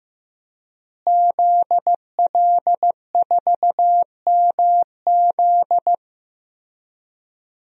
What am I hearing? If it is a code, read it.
Morse code, heard as a keyed tone, about 15 wpm, 700 Hz. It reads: ZL4MZ
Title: DXCC-100 Call Signs - 5 Characters - Repeated 2x - 15wpm